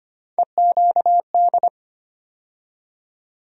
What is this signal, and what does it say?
Morse code, heard as a keyed tone, about 25 wpm, 700 Hz. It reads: EQD
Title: USA Call Sign Postfixes - 3 Letters 25wpm